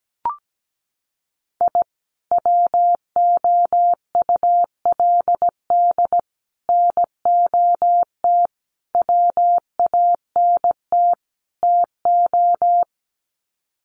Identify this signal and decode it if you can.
Morse code, heard as a keyed tone, about 17 wpm, 700 Hz. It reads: I WOULD NOT WANT TO
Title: Sets of 5 Words 17wpm